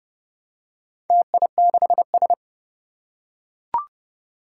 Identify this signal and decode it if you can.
Morse code, heard as a keyed tone, about 30 wpm, 700 Hz. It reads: TI6S
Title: Four Challenging Letter-Number Combinations 30wpm